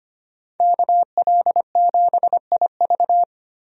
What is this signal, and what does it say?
Morse code, heard as a keyed tone, about 25 wpm, 700 Hz. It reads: KL7IV
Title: DXCC-100 Call Signs - 5 Characters - 25wpm